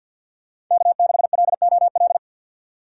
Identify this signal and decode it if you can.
Morse code, heard as a keyed tone, about 53 wpm, 700 Hz. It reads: K6LOL
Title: USA Call Signs - 5 Characters - 2x Character Spacing - 50wpm